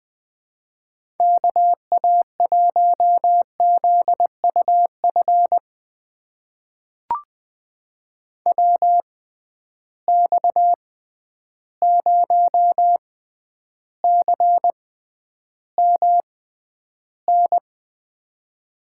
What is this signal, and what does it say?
Morse code, heard as a keyed tone, about 20 wpm, spaced as for 10 wpm, 700 Hz. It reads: KA1ZUF WX0CMN
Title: USA Call Signs - 6 Characters - 6x Character Spacing - 20wpm